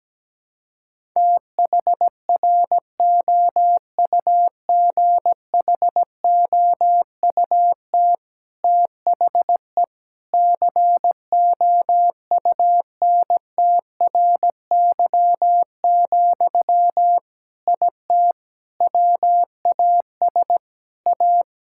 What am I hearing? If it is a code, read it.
Morse code, heard as a keyed tone, about 17 wpm, 700 Hz. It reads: THROUGHOUT THE COUNTRY, IT WAS A
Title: Sentences from Top 1000 Words - Review Part 5 - 17wpm